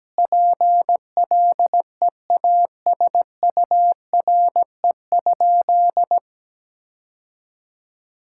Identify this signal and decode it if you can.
Morse code, heard as a keyed tone, about 17 wpm, 700 Hz. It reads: PLEASURE?
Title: Sentences from Top 1600 Words 17wpm